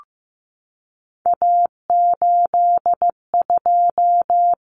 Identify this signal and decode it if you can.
Morse code, heard as a keyed tone, about 15 wpm, 700 Hz. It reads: A82
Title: Common Cut Numbers - 3-Digit 15wpm